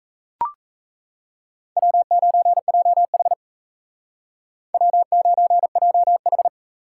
Morse code, heard as a keyed tone, about 35 wpm, 700 Hz. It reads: W9JH W9JH